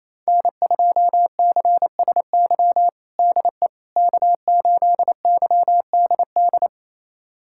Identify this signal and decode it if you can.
Morse code, heard as a keyed tone, about 28 wpm, 700 Hz. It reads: N2CSY DE K8YDD
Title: USA Callsign DE Callsign - 5 Characters - 28wpm